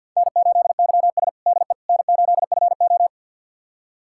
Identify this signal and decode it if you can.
Morse code, heard as a keyed tone, about 47 wpm, 700 Hz. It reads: N8YS DE N8FO
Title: USA Callsign DE Callsign - 4 Characters - 50wpm